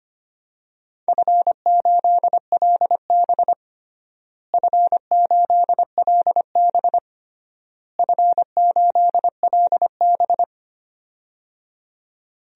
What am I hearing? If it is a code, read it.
Morse code, heard as a keyed tone, about 25 wpm, 700 Hz. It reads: F8LB F8LB F8LB
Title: DXCC-100 Call Signs - 4 Characters - Repeated 3x - 25wpm